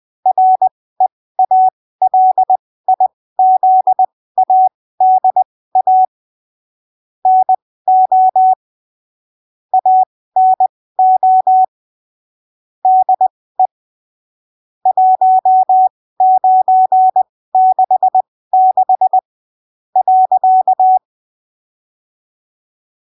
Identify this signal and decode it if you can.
Morse code, heard as a keyed tone, about 20 wpm, spaced as for 15 wpm, 750 Hz. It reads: REALIZADA NO ANO DE 1966 .
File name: arquivos / texto / texto 07 20ppm